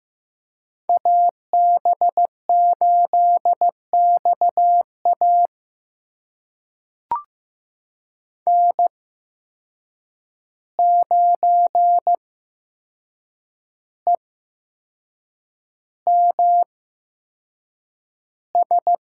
Morse code, heard as a keyed tone, about 15 wpm, spaced as for 6 wpm, 700 Hz. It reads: AB8XA N9EMS
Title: USA Call Signs - 8x Character Spacing - 15wpm